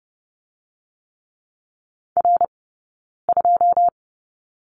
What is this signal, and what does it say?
Morse code, heard as a keyed tone, about 30 wpm, 700 Hz. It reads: R 2
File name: Lesson 16d - Letter c - Two Characters 30wpm